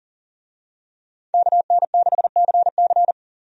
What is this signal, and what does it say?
Morse code, heard as a keyed tone, about 40 wpm, 700 Hz. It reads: KN6CC
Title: USA Call Signs - 5 Characters - 40wpm